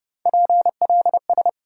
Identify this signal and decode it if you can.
Morse code, heard as a keyed tone, about 34 wpm, 700 Hz. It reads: PLS